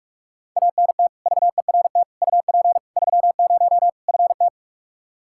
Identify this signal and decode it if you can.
Morse code, heard as a keyed tone, about 41 wpm, 700 Hz. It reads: ANT VERT UP 30 FT
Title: K1USN SST QSO Practice 45wpm